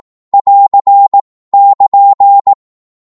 Morse code, heard as a keyed tone, about 18 wpm, 800 Hz. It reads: <AR> <KN>
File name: Morse / qso-18wpm